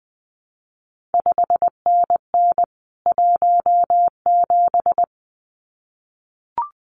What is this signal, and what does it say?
Morse code, heard as a keyed tone, about 19 wpm, 700 Hz. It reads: 5NN 17
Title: CQ WW DX Contest Exchange Encore - 20wpm